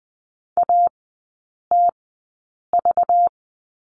Morse code, heard as a keyed tone, about 20 wpm, spaced as for 8 wpm, 700 Hz. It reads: ATV
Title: 205 Lesson 33e - Letter j - Three Characters 20wpm